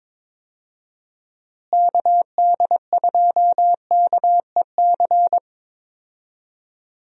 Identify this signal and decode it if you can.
Morse code, heard as a keyed tone, about 22 wpm, 700 Hz. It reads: KD2KEC
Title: USA Call Signs - 6 Characters - Repeated 2x - 22wpm